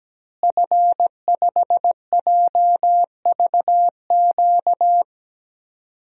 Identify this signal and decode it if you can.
Morse code, heard as a keyed tone, about 17 wpm, 700 Hz. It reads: F5JVQ
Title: DXCC-100 Call Signs - 5 Characters - Repeated 2x - 17wpm